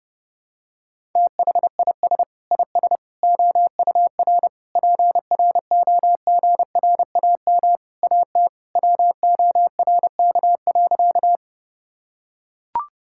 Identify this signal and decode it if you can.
Morse code, heard as a keyed tone, about 30 wpm, 700 Hz. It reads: THIS IS OUR PROGRAM AT WORK.
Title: Sentences from Top 1000 Words - Review Part 5 - 30wpm